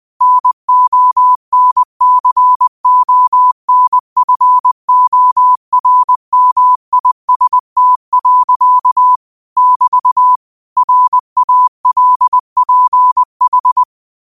Morse code, heard as a keyed tone, about 20 wpm, 1000 Hz. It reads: NONCONFORMIST. <BT> RALPH
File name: qotdfiles / 20wpm